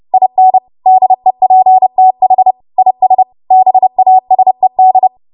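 Morse code, heard as a keyed tone, about 30 wpm, 750 Hz. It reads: IN DEPTH IS BASED
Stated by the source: W1AW Web Code Practice Files, 30 WPM run from ARRL; text from April 2023 QST.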